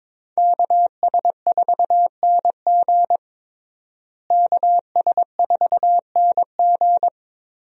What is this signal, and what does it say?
Morse code, heard as a keyed tone, about 22 wpm, 700 Hz. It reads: KS4NG KS4NG
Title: USA Call Signs - 5 Characters - Repeated 2x - 22wpm